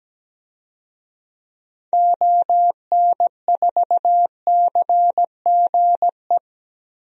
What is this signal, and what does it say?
Morse code, heard as a keyed tone, about 17 wpm, 700 Hz. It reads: ON4CGE
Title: DXCC-100 Call Signs - 6 Characters - Repeated 2x - 17wpm